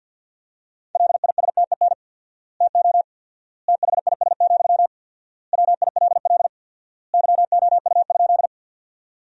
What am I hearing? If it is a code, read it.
Morse code, heard as a keyed tone, about 50 wpm, 700 Hz. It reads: LISTEN TO THIS, WILL YOU?
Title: 4x Word-Spacing - Sentences from Top 400 Words - 50wpm